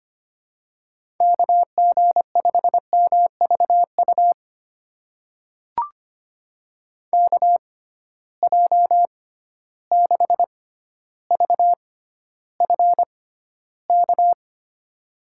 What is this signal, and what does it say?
Morse code, heard as a keyed tone, about 25 wpm, spaced as for 13 wpm, 700 Hz. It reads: KG5MVU KJ6VFK
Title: USA Call Signs - 6x Character Spacing - 25wpm